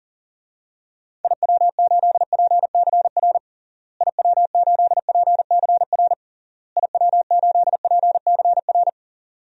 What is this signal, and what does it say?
Morse code, heard as a keyed tone, about 40 wpm, 700 Hz. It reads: IW8PCR IW8PCR IW8PCR
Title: DXCC-100 Call Signs - 6 Characters - Repeated 3x - 40wpm